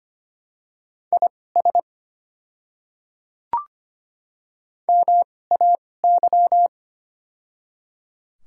Morse code, heard as a keyed tone, about 25 wpm, spaced as for 18 wpm, 700 Hz. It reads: IS MAY
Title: 2x Character Spacing - Top 100 Words - 25wpm